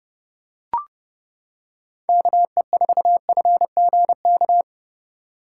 Morse code, heard as a keyed tone, about 29 wpm, 700 Hz. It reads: KE4FGK